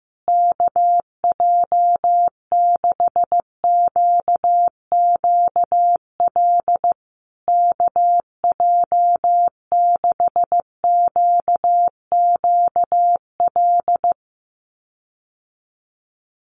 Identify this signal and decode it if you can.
Morse code, heard as a keyed tone, about 15 wpm, 700 Hz. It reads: KJ6QQL KJ6QQL
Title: USA Call Signs Repeated 3x 15wpm